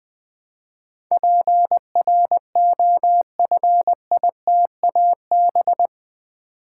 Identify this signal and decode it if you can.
Morse code, heard as a keyed tone, about 20 wpm, 700 Hz. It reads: PROFITAB
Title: Build A Word - 4 Letters At A Time - 10-Letter Words - 20wpm